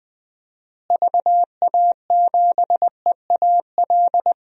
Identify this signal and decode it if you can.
Morse code, heard as a keyed tone, about 21 wpm, 700 Hz. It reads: VA7EAL
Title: DXCC-100 Call Signs - 6 Characters - Repeated 3x - 20wpm